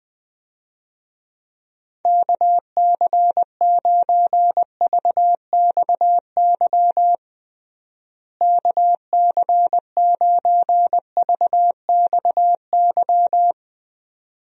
Morse code, heard as a keyed tone, about 21 wpm, 700 Hz. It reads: KC9VXY KC9VXY
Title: USA Call Signs - 6 Characters - Repeated 2x - 20wpm